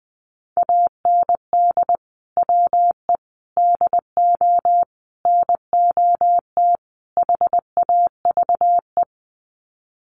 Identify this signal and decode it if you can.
Morse code, heard as a keyed tone, about 20 wpm, 700 Hz. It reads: AND WE DO NOT HAVE
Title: Sets of 5 Words Encore 20wpm